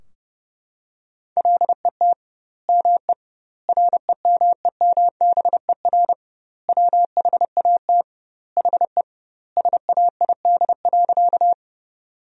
Morse code, heard as a keyed tone, about 30 wpm, 700 Hz. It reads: LET ME REMEMBER WHAT HE SAID.
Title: 2x Word-Spacing - Sentences from Top 400 Words - 30wpm